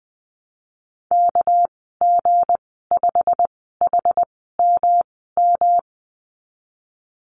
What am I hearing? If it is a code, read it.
Morse code, heard as a keyed tone, about 20 wpm, spaced as for 15 wpm, 700 Hz. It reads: KG5HMM
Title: 207 Lesson 33g - Letter j - Call Signs 20wpm